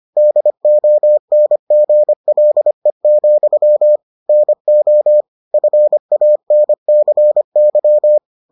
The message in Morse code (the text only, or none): DONGLE, NO FANCY